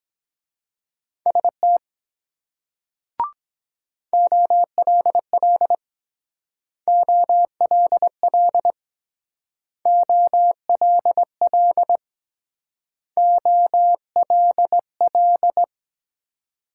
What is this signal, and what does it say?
Morse code, heard as a keyed tone, about 23 wpm, 700 Hz. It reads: ST OLL OLL OLL OLL